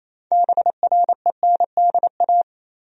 Morse code, heard as a keyed tone, about 28 wpm, 700 Hz. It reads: BRENDA
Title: USA Common Names - 28wpm